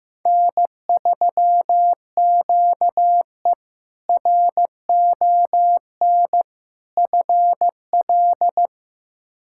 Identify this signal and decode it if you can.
Morse code, heard as a keyed tone, about 15 wpm, 700 Hz. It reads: N3QE RON FL
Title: K1USN SST QSO Practice - 15wpm